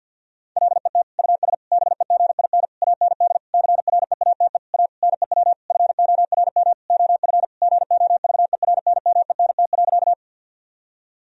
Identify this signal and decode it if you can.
Morse code, heard as a keyed tone, about 50 wpm, 700 Hz. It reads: LET US BEGIN AND CREATE A NEW FORM OF GOVERNMENT.
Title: Sentences from Top 1000 Words - Review Part 3 - 50wpm